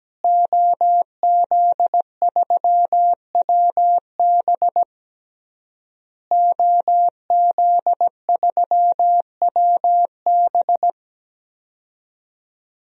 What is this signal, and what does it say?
Morse code, heard as a keyed tone, about 17 wpm, 700 Hz. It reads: OZ3WB OZ3WB
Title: DXCC-100 Call Signs - 5 Characters - Repeated 3x - 17wpm